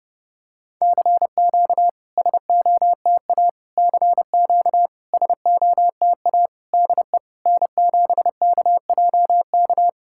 Morse code, heard as a keyed tone, about 30 wpm, 690 Hz. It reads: CQ SOTA CQ SOTA DE N7KJK
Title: SOTA QSO Practice 30wpm